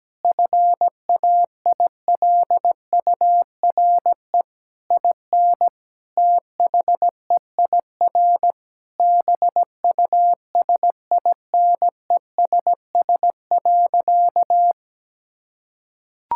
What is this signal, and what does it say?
Morse code, heard as a keyed tone, about 17 wpm, 700 Hz. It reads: FAILURE IN THEIR BUSINESS.
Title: Sentences from Top 1000 Words - Review Part 6 - 17wpm